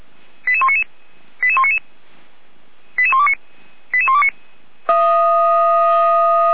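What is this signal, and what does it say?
Five-tone selective call (ZVEI), heard as quick runs of five tones, tones of about 70 ms. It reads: 88199 88199 88129 88129
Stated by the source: ZVEI five-tone selective call received with an SDR for Feuerwehr Orlamünde, a fire brigade.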